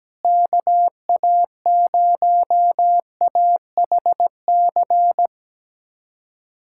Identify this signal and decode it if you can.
Morse code, heard as a keyed tone, about 17 wpm, 700 Hz. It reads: KA0AHC
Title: USA Callsign DE Callsign 17wpm